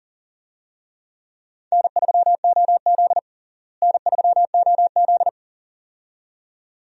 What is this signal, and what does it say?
Morse code, heard as a keyed tone, about 40 wpm, 700 Hz. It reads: N3OZ N3OZ
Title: USA Call Signs - 4 Characters - Repeated 2x - 40wpm